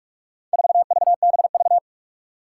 Morse code, heard as a keyed tone, about 44 wpm, 700 Hz. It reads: 4VBV
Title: Four Challenging Letter-Number Combinations 45wpm